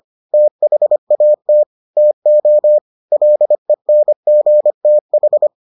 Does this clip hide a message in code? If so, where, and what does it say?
Yes: Morse code from 0.3 to 5.5 s, heard as a keyed tone, about 25 wpm, 600 Hz. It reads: THAT TO LENGTH